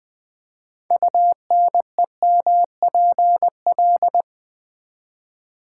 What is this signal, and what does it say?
Morse code, heard as a keyed tone, about 20 wpm, 700 Hz. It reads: UNEMPL
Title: Build A Word - 6 Letters At A Time - 12-Letter Words - 20wpm